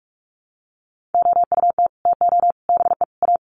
Morse code, heard as a keyed tone, about 44 wpm, 700 Hz. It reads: OUT TO BE A
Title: Sets of 4 Words 45wpm